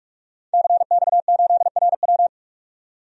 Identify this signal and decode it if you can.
Morse code, heard as a keyed tone, about 46 wpm, 700 Hz. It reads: CX8RW
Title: DXCC-100 Call Signs - 5 Characters - 45wpm